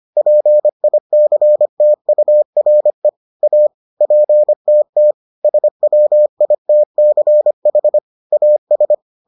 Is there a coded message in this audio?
Yes: Morse code from 0.2 to 8.9 s, heard as a keyed tone, about 25 wpm, 600 Hz. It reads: PICTURE A PTT SWITCH AS